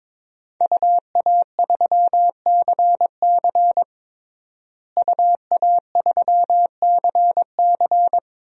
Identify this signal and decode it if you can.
Morse code, heard as a keyed tone, about 22 wpm, 700 Hz. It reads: UA3CC UA3CC
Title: DXCC-100 Call Signs - 5 Characters - Repeated 2x - 22wpm